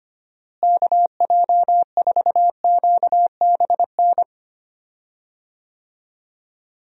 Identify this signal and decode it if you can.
Morse code, heard as a keyed tone, about 25 wpm, 700 Hz. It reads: KJ4QBN
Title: USA Call Signs - 6 Characters - Repeated 2x - 25wpm